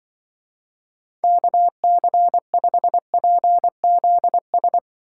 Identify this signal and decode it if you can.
Morse code, heard as a keyed tone, about 24 wpm, 700 Hz. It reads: KC5PZS
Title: Speed Racing - USA 2x3 Call Signs 20wpm